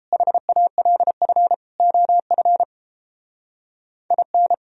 Morse code, heard as a keyed tone, about 29 wpm, 700 Hz. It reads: HALF OF IN